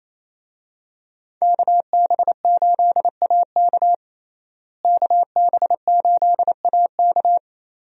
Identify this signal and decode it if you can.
Morse code, heard as a keyed tone, about 28 wpm, 700 Hz. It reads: KB8AK KB8AK
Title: USA Call Signs - 5 Characters - Repeated 3x - 28wpm